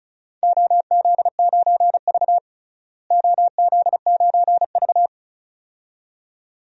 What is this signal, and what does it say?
Morse code, heard as a keyed tone, about 36 wpm, 700 Hz. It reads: OZ9V OZ9V
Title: DXCC-100 Call Signs - 4 Characters - Repeated 2x - 35wpm